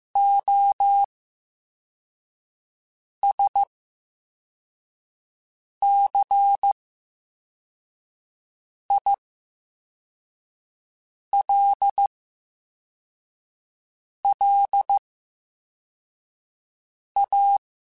Morse code, heard as a keyed tone, about 15 wpm, spaced as for 4 wpm, 800 Hz. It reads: OSCILLA
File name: Special INGs-mp3s / Special INGs-mp3s / VerySlowToFaster Ing @13